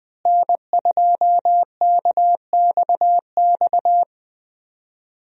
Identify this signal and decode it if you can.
Morse code, heard as a keyed tone, about 21 wpm, 700 Hz. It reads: N2KXX
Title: USA Call Signs - 5 Characters - Repeated 3x - 20wpm